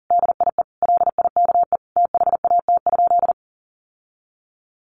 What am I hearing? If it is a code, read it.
Morse code, heard as a keyed tone, about 40 wpm, 700 Hz. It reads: DIE LIKE THAT?
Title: Sentences from Top 1400 Words 40wpm